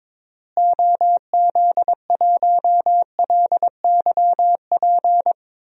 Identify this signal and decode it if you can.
Morse code, heard as a keyed tone, about 21 wpm, 700 Hz. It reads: OZ1LYP
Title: DXCC-100 Call Signs - 6 Characters - Repeated 3x - 22wpm